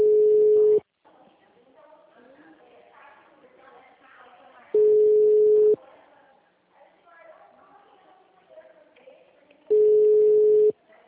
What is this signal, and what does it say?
Ringback tone (Europe, 425 Hz), with speech.
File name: out- -2007- - - .59452